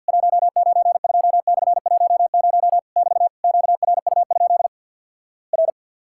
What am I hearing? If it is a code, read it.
Morse code, heard as a keyed tone, about 50 wpm, 700 Hz. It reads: 192/10 <BT> QRU? R
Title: Worked All Europe DX Contest - QTC Exchange with Repeats - 50wpm